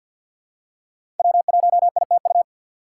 Morse code, heard as a keyed tone, about 53 wpm, 700 Hz. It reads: W1ITU
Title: USA Call Signs - 4x Character Spacing - 50wpm